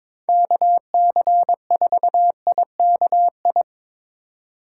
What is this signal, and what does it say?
Morse code, heard as a keyed tone, about 22 wpm, 700 Hz. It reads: KC4IKI
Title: USA Call Signs - 2x Character Spacing - 22wpm